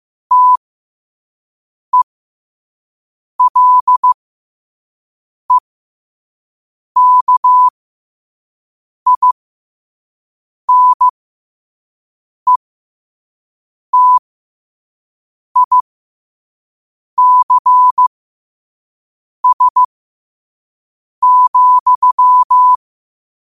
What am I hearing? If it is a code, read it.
Morse code, heard as a keyed tone, about 15 wpm, spaced as for 5 wpm, 1000 Hz. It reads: TELEKINETICS,
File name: qotdfiles / 5wpm